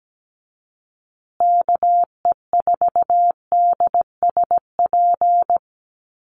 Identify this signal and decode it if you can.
Morse code, heard as a keyed tone, about 17 wpm, 700 Hz. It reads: KE4DSP
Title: USA Call Signs Repeated 3x 17wpm